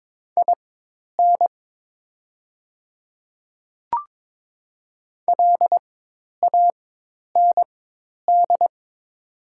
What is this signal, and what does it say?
Morse code, heard as a keyed tone, about 22 wpm, spaced as for 10 wpm, 700 Hz. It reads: IN LAND
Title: 4x Character Spacing - Top 200 Words - 22wpm